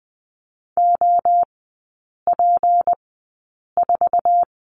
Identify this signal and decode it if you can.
Morse code, heard as a keyed tone, about 20 wpm, spaced as for 8 wpm, 700 Hz. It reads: OP4
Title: 205 Lesson 33e - Letter j - Three Characters 20wpm